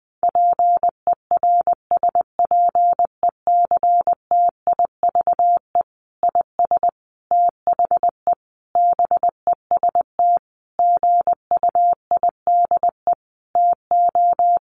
Morse code, heard as a keyed tone, about 20 wpm, 700 Hz. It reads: PERSPECTIVE IS THE BEST GUIDE TO
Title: Sentences from Top 1200 Words 20wpm